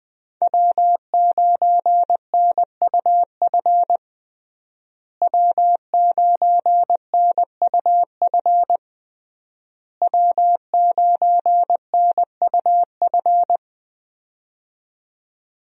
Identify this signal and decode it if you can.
Morse code, heard as a keyed tone, about 20 wpm, 700 Hz. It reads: W9NUF W9NUF W9NUF
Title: USA Call Signs - 5 Characters - Repeated 3x - 20wpm